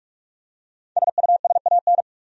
Morse code, heard as a keyed tone, about 46 wpm, 700 Hz. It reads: SUSAN
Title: USA Common Names - Max 5 Letters - 45wpm